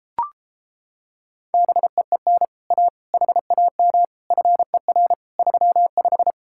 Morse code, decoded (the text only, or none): BEEN A HAM FER 35